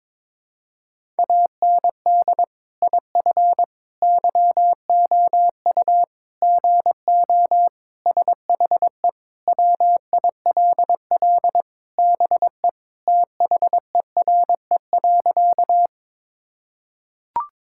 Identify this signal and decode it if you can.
Morse code, heard as a keyed tone, about 22 wpm, 700 Hz. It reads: AND IF YOU GO SHE WILL BE THERE.
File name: practice / 22wpm / Sentences with 3-Word Phrases - 22wpm